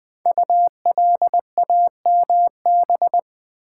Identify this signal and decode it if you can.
Morse code, heard as a keyed tone, about 20 wpm, 700 Hz. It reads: ULAMB
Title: Code Groups - 5 Letters - 20wpm